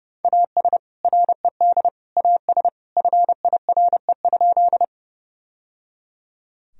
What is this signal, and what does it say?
Morse code, heard as a keyed tone, about 30 wpm, 700 Hz. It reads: AS RED AS FIRE?
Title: Sentences from Top 1000 Words - Review Part 2 - 30wpm